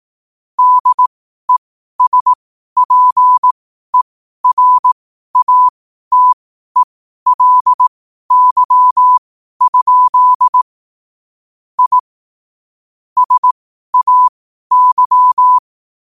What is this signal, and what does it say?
Morse code, heard as a keyed tone, about 18 wpm, spaced as for 12 wpm, 1000 Hz. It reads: DESPERATELY? I SAY
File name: qotdfiles / 12wpm